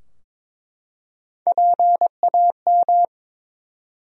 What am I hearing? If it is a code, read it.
Morse code, heard as a keyed tone, about 22 wpm, 700 Hz. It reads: PAM
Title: USA Common Names - Max 4 Letters - 22wpm